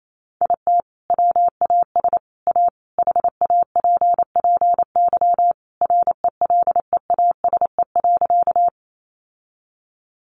Morse code, heard as a keyed tone, about 28 wpm, 700 Hz. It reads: IT WAS A HAPPY RELEASE.
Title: Sentences from Top 1100 Words 28wpm